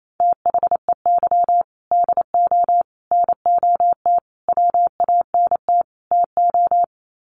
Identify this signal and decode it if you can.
Morse code, heard as a keyed tone, about 28 wpm, 700 Hz. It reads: THEY DO NOT WANT TO
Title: Sets of 5 Words 28wpm